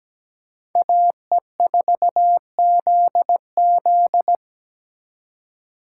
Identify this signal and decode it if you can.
Morse code, heard as a keyed tone, about 20 wpm, 700 Hz. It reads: AE4ZZ
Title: Speed Racing - USA 2x2 Call Signs 17wpm